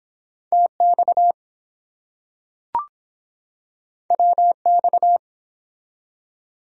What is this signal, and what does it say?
Morse code, heard as a keyed tone, about 23 wpm, 700 Hz. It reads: TX WX